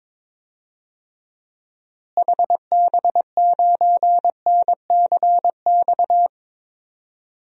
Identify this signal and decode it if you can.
Morse code, heard as a keyed tone, about 22 wpm, 700 Hz. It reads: HB9NCX
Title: DXCC-100 Call Signs - 6 Characters - Repeated 2x - 22wpm